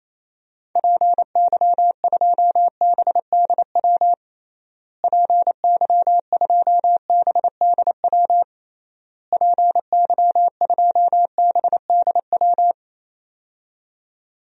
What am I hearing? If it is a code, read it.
Morse code, heard as a keyed tone, about 28 wpm, 700 Hz. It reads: PY2BDW PY2BDW PY2BDW
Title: DXCC-100 Call Signs - 6 Characters - Repeated 3x - 28wpm